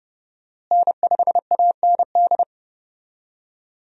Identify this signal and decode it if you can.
Morse code, heard as a keyed tone, about 32 wpm, 700 Hz. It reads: N5AND
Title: USA Call Signs - 5 Characters - Repeated 2x - 30wpm